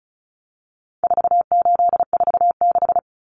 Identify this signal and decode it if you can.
Morse code, heard as a keyed tone, about 34 wpm, 700 Hz. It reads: 4846
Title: Common Cut Numbers - 4-Digit 35wpm